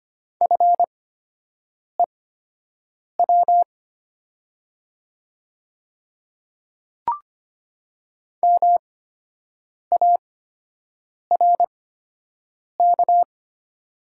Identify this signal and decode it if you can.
Morse code, heard as a keyed tone, about 25 wpm, spaced as for 7 wpm, 700 Hz. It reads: FEW MARK
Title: 8x Character Spacing - Top 500 Words - 25wpm